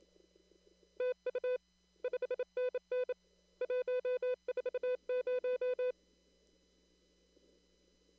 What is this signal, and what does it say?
Morse code, heard as a keyed tone, about 28 wpm, 500 Hz. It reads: TU 5NN 140
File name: CW / kt5j wpx cw-op-k5pi